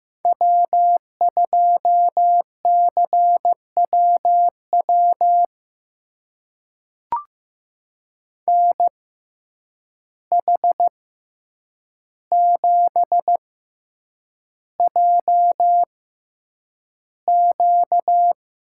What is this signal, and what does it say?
Morse code, heard as a keyed tone, about 15 wpm, spaced as for 8 wpm, 700 Hz. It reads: W2CWW NH7JQ
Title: USA Call Signs - 5 Characters - 6x Character Spacing - 15wpm